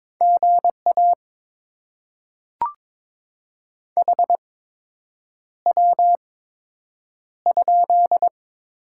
Morse code, heard as a keyed tone, about 21 wpm, spaced as for 8 wpm, 700 Hz. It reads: GA HW?
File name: Top 20 QSO Elements - 8x Character Spacing - 22wpm